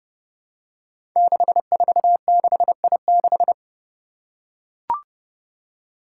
Morse code, heard as a keyed tone, about 28 wpm, 700 Hz. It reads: 646I6
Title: Five Challenging Letter-Number Combinations 30wpm